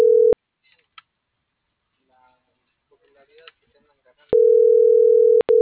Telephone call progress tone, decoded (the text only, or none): ringback tone